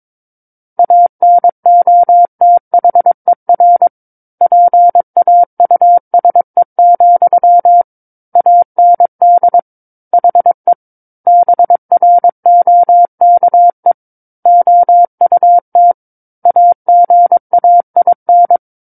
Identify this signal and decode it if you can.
Morse code, heard as a keyed tone, about 22 wpm, 700 Hz. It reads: ANOTHER PAUSE, AND HE BROKE OUT AGAIN
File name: Book One / 13-Chapter 13 - 22 WPM